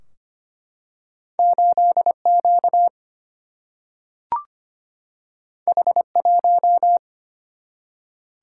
Morse code, heard as a keyed tone, about 25 wpm, 700 Hz. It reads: 8 Q H 1
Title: Single-Letter Single-Number 25wpm